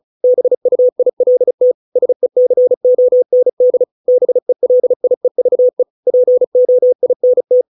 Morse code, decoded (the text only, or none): BUILT SECOND BELIEVE POINT